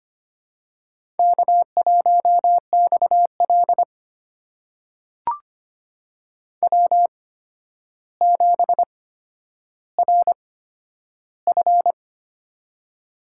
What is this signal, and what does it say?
Morse code, heard as a keyed tone, about 25 wpm, spaced as for 11 wpm, 700 Hz. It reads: K1XL W7RF
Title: USA Call Signs - 4 Characters - 8x Character Spacing - 25wpm